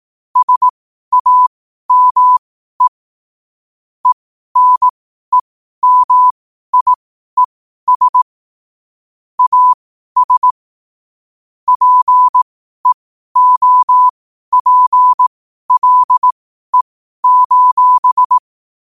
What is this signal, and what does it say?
Morse code, heard as a keyed tone, about 18 wpm, spaced as for 12 wpm, 1000 Hz. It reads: SAME ENEMIES AS PEOPLE:
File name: qotdfiles / 12wpm